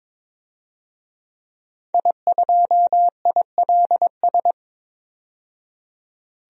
Morse code, heard as a keyed tone, about 22 wpm, 700 Hz. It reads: I2ILS
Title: DXCC-100 Call Signs - 5 Characters - 22wpm